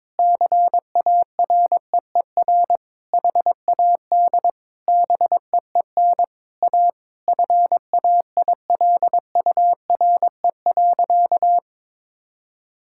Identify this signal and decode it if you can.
Morse code, heard as a keyed tone, about 22 wpm, 700 Hz. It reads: CAREER HAD BEEN A FAILURE.
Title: Sentences from Top 1000 Words - Review Part 6 - 22wpm